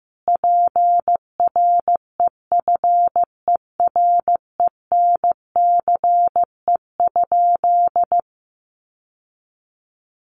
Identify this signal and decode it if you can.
Morse code, heard as a keyed tone, about 15 wpm, 700 Hz. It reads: PREFERENCE?
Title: Sentences from Top 1500 Words 15wpm